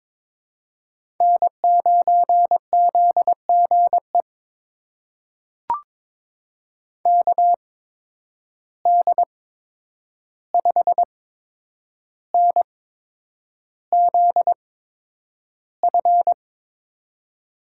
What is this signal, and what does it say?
Morse code, heard as a keyed tone, about 22 wpm, spaced as for 9 wpm, 700 Hz. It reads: N9ZGE KD5NZF
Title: USA Call Signs - 8x Character Spacing - 22wpm